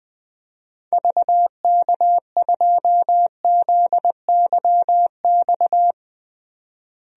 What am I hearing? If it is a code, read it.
Morse code, heard as a keyed tone, about 20 wpm, 700 Hz. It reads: VK2ZYX
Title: DXCC-100 Call Signs - 6 Characters - Repeated 3x - 20wpm